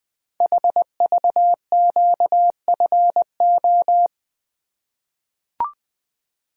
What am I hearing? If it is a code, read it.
Morse code, heard as a keyed tone, about 20 wpm, 700 Hz. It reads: HVQFO
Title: Code Groups - 5 Letters - 20wpm